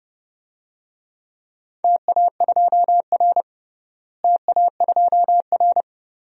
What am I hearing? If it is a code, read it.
Morse code, heard as a keyed tone, about 30 wpm, 700 Hz. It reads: TA2R TA2R
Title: DXCC-100 Call Signs - 4 Characters - Repeated 2x - 30wpm